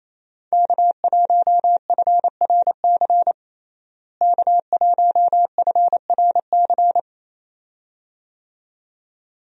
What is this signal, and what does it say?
Morse code, heard as a keyed tone, about 28 wpm, 700 Hz. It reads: K1FRC K1FRC
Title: USA Call Signs - 5 Characters - Repeated 2x - 28wpm